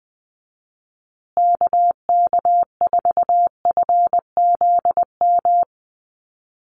Morse code, heard as a keyed tone, about 20 wpm, 700 Hz. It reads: KK4FZM
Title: USA Call Signs Repeated 2x 20wpm